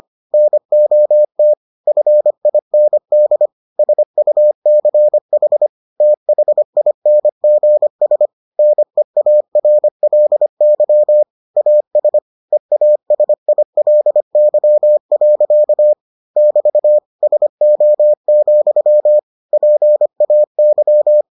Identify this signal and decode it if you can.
Morse code, heard as a keyed tone, about 25 wpm, 600 Hz. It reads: NOT FIND SUCH THINGS NEARLY AS EASILY. <BT> SO, PAY